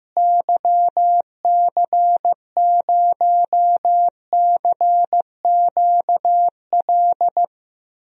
Morse code, heard as a keyed tone, about 15 wpm, 700 Hz. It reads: YC0CQL
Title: DXCC-100 Call Signs - 6 Characters - Repeated 3x - 15wpm